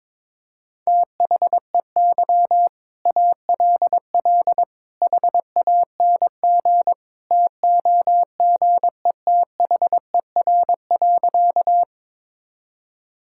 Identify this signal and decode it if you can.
Morse code, heard as a keyed tone, about 22 wpm, 700 Hz. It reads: THEY ALL HANG TOGETHER.
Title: Sentences from Top 1600 Words 22wpm